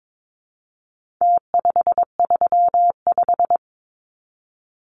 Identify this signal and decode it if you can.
Morse code, heard as a keyed tone, about 22 wpm, 700 Hz. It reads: T535
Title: Numbers - Contest Style - 4-Digit 22wpm